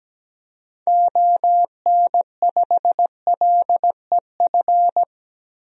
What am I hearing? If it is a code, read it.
Morse code, heard as a keyed tone, about 17 wpm, 700 Hz. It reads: ON5LEF
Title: DXCC-100 Call Signs - 6 Characters - 17wpm